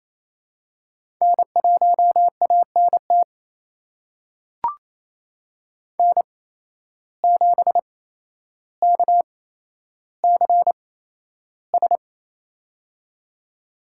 Morse code, heard as a keyed tone, about 28 wpm, spaced as for 12 wpm, 700 Hz. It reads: N1ANT N7KCS
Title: USA Call Signs - 5 Characters - 8x Character Spacing - 28wpm